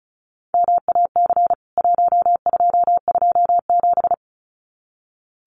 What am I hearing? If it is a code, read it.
Morse code, heard as a keyed tone, about 34 wpm, 700 Hz. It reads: MAC 1227
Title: CWT Exchange 35wpm